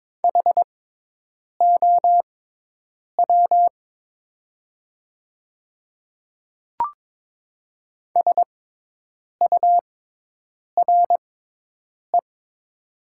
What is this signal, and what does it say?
Morse code, heard as a keyed tone, about 22 wpm, spaced as for 8 wpm, 700 Hz. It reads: HOW SURE
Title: 6x Character Spacing - Top 300 Words - 22wpm